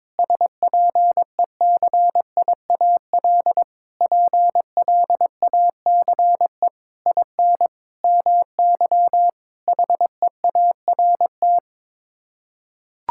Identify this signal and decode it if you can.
Morse code, heard as a keyed tone, about 22 wpm, 700 Hz. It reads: SPECIAL PLACE IN MY HEART
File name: Sets of 9 Words - 22wpm